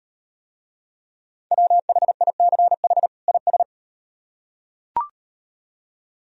Morse code, heard as a keyed tone, about 38 wpm, 700 Hz. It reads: WHICH IS